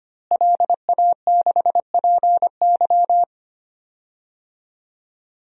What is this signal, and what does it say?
Morse code, heard as a keyed tone, about 25 wpm, 700 Hz. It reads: LA6PY
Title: DXCC-100 Call Signs - 5 Characters - Repeated 2x - 25wpm